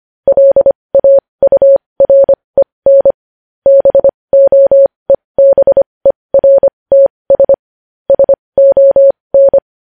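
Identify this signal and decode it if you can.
Morse code, heard as a keyed tone, about 25 wpm, spaced as for 20 wpm, 550 Hz. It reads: LAUREN BOEBERTS SON
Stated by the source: Morse code news headlines.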